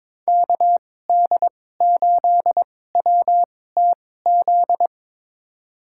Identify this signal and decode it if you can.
Morse code, heard as a keyed tone, about 22 wpm, spaced as for 16 wpm, 700 Hz. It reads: KD8WTZ
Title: USA Call Signs - 2x Character Spacing - 22wpm